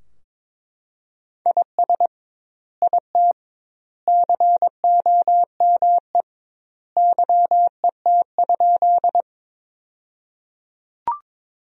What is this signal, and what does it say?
Morse code, heard as a keyed tone, about 21 wpm, 700 Hz. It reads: IS IT COME YET?
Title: 2x Word-Spacing - Sentences from Top 500 Words - 22wpm